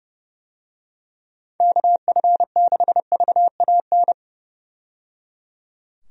Morse code, heard as a keyed tone, about 30 wpm, 700 Hz. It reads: KF6VAN
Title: USA Call Signs - 6 Characters - 30wpm